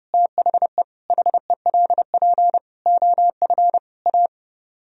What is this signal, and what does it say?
Morse code, heard as a keyed tone, about 32 wpm, 700 Hz. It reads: THE HELP OF A